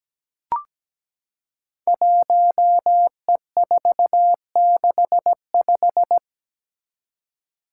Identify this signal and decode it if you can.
Morse code, heard as a keyed tone, about 17 wpm, 700 Hz. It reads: 1E465
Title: Code Groups - 5 Letter:Numbers - 17wpm